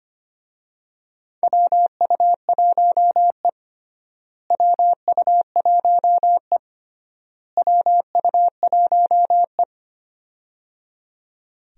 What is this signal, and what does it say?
Morse code, heard as a keyed tone, about 25 wpm, 700 Hz. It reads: WU1E WU1E WU1E
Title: USA Call Signs - 4 Characters - Repeated 3x - 25wpm